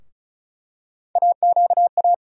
Morse code, heard as a keyed tone, about 36 wpm, 700 Hz. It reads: AQA
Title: USA Call Sign Postfixes - 3 Letters 35wpm